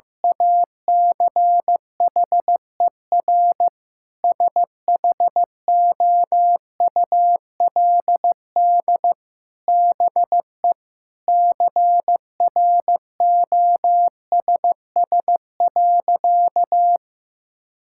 Morse code, heard as a keyed tone, about 16 wpm, 700 Hz. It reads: ACHER SHOULD BE CROSS.
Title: Sentences from Top 1000 Words - Review Part 6 - 15wpm